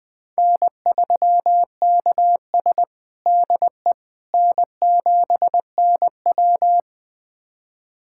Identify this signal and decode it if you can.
Morse code, heard as a keyed tone, about 21 wpm, 700 Hz. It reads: N3KS DE N7NW
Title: USA Callsign DE Callsign - 4 Characters - 20wpm